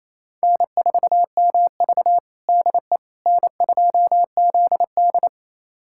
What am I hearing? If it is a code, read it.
Morse code, heard as a keyed tone, about 28 wpm, 700 Hz. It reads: N4MV DE N2ZD
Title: USA Callsign DE Callsign - 4 Characters - 28wpm